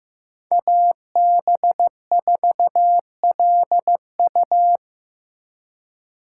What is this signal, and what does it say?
Morse code, heard as a keyed tone, about 18 wpm, 700 Hz. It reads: AB4LU
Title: Speed Racing - USA 2x2 Call Signs 15wpm